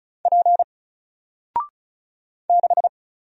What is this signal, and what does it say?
Morse code, heard as a keyed tone, about 33 wpm, 700 Hz. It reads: P 6
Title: Single Letter-Number Alt 35wpm